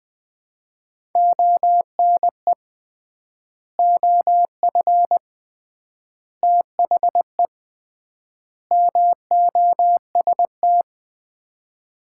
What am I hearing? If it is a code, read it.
Morse code, heard as a keyed tone, about 20 wpm, 700 Hz. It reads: ONE OF THE MOST
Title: 3x Word-Spacing - Sets of 4 Words - 20wpm